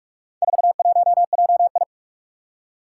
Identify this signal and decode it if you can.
Morse code, heard as a keyed tone, about 44 wpm, 700 Hz. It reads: 41JI
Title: Four Challenging Letter-Number Combinations 45wpm